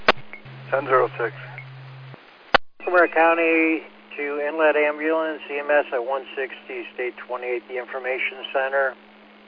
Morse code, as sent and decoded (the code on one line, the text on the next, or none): none